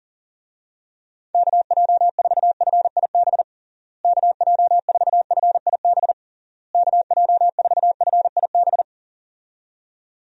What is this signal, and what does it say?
Morse code, heard as a keyed tone, about 37 wpm, 700 Hz. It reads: KJ4FIB KJ4FIB KJ4FIB
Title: USA Call Signs - 6 Characters - Repeated 3x - 40wpm